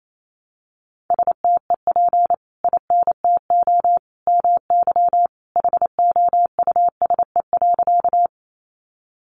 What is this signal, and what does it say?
Morse code, heard as a keyed tone, about 28 wpm, 700 Hz. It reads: STEP INTO MY HOUSE.
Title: Sentences from Top 600 Words 28wpm